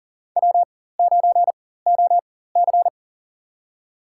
Morse code, heard as a keyed tone, about 40 wpm, spaced as for 19 wpm, 700 Hz. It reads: W9OC
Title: USA Call Signs - 4x Character Spacing - 40wpm